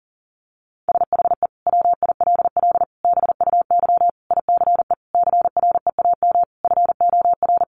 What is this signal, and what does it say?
Morse code, heard as a keyed tone, about 40 wpm, 700 Hz. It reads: SHE WILL BUY ICE CREAM FOR
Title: Sentences from Top 1500 Words 40wpm